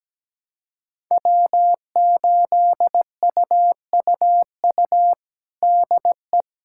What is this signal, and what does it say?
Morse code, heard as a keyed tone, about 17 wpm, 700 Hz. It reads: W8UUU DE
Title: USA Callsign DE Callsign - 5 Characters - 17wpm